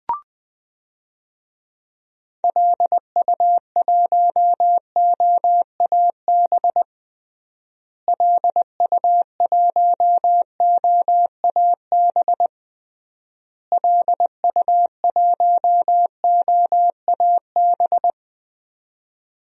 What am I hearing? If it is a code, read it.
Morse code, heard as a keyed tone, about 19 wpm, 700 Hz. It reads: LU1OAB LU1OAB LU1OAB
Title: DXCC-100 Call Signs - 6 Characters - Repeated 3x - 20wpm